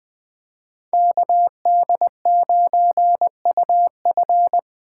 Morse code, heard as a keyed tone, about 21 wpm, 700 Hz. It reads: KD9UF
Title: Speed Racing - USA 2x2 Call Signs 17wpm